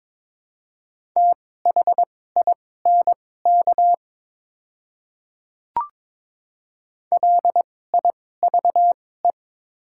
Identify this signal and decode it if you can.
Morse code, heard as a keyed tone, about 22 wpm, spaced as for 16 wpm, 700 Hz. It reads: THINK LIVE
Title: 2x Character Spacing - Top 500 Words - 22wpm